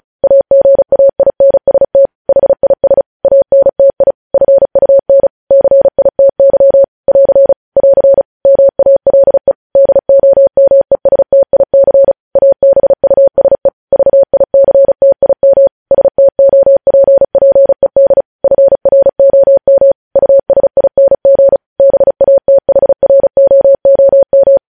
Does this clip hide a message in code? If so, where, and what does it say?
Yes: Morse code from 0.2 to 24.6 s, heard as a keyed tone, about 36 wpm, 550 Hz. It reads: AGAINST HIS ANTI FUN CITY <AR> <AR> MALE DOMESTIC ABUSE VICTIM STOPPED FROM USING BATHROOM